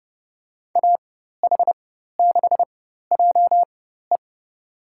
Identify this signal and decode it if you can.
Morse code, heard as a keyed tone, about 30 wpm, spaced as for 14 wpm, 700 Hz. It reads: AH6JE
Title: USA Call Signs - 5 Characters - 4x Character Spacing - 30wpm